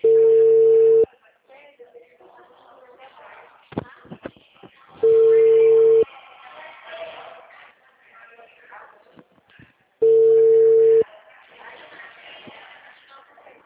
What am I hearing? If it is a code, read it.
Ringback tone (China, 450 Hz), with speech.